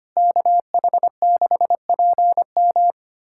Morse code, heard as a keyed tone, about 25 wpm, 700 Hz. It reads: KH6PM
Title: USA Callsign DE Callsign - 5 Characters - 25wpm